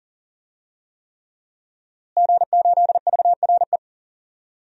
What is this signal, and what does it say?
Morse code, heard as a keyed tone, about 40 wpm, 700 Hz. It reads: G8VRE
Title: DXCC-100 Call Signs - 5 Characters - Repeated 3x - 40wpm